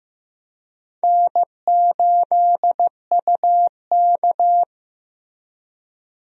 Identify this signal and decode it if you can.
Morse code, heard as a keyed tone, about 15 wpm, 700 Hz. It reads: N8UK
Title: USA Call Signs - 4 Characters - Repeated 3x - 15wpm